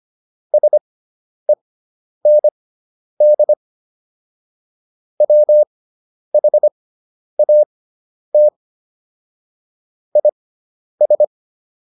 Morse code, heard as a keyed tone, about 25 wpm, spaced as for 10 wpm, 600 Hz. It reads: SEND WHAT IS